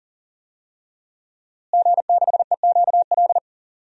Morse code, heard as a keyed tone, about 40 wpm, 700 Hz. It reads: G6EQL
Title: DXCC-100 Call Signs - 5 Characters - Repeated 3x - 40wpm